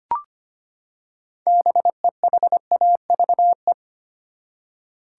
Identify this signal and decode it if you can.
Morse code, heard as a keyed tone, about 25 wpm, 700 Hz. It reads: BEHAVE
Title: Six Letter Words - 25wpm